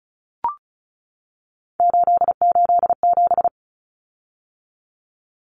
Morse code, heard as a keyed tone, about 34 wpm, 700 Hz. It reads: 887
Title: Common Cut Numbers - 3-Digit 35wpm